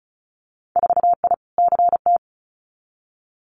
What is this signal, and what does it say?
Morse code, heard as a keyed tone, about 36 wpm, 700 Hz. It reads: 4I CT
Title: Winter Field Day Exchanges Encore 35wpm